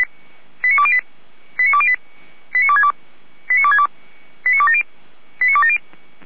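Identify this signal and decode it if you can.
Five-tone selective call (ZVEI), heard as quick runs of five tones, tones of about 70 ms. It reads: 89208 89208 89262 89262 89270 89270
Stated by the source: ZVEI five-tone call received on an SDR.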